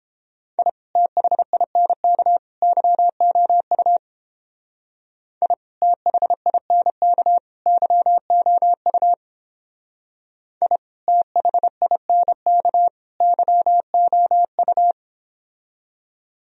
Morse code, heard as a keyed tone, about 29 wpm, 700 Hz. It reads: I THINK YOU I THINK YOU I THINK YOU